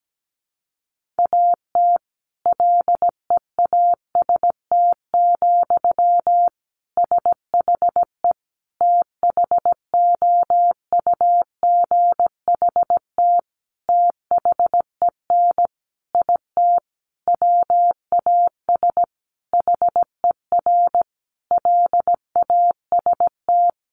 Morse code, heard as a keyed tone, about 17 wpm, 700 Hz. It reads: AT LEAST, SHE THOUGHT THEN IT WAS HER LAST
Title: Sentences from Top 1400 Words 17wpm